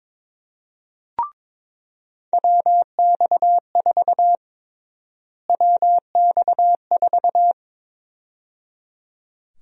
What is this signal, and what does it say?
Morse code, heard as a keyed tone, about 22 wpm, 700 Hz. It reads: WX4 WX4
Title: USA Call Sign Prefixes - Repeated 2x - 22wpm